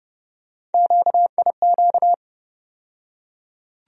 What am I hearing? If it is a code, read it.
Morse code, heard as a keyed tone, about 30 wpm, 700 Hz. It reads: QIQ
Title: USA Call Sign Postfixes - 3 Letters 30wpm